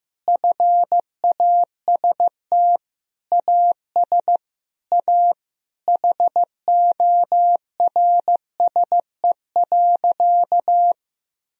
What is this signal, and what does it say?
Morse code, heard as a keyed tone, about 15 wpm, 700 Hz. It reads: FAST AS A HORSE.
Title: Sentences from Top 1000 Words - Review Part 2 - 15wpm